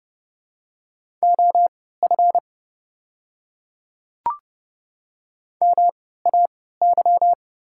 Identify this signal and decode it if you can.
Morse code, heard as a keyed tone, about 30 wpm, spaced as for 17 wpm, 700 Hz. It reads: OF MAY
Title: 3x Character Spacing - Top 100 Words - 30wpm